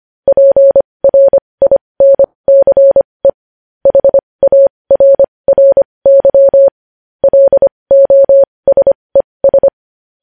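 Morse code, heard as a keyed tone, about 25 wpm, spaced as for 20 wpm, 550 Hz. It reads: PRINCE HARRY LOSES
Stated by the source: Morse code news headlines sent at 20 wpm.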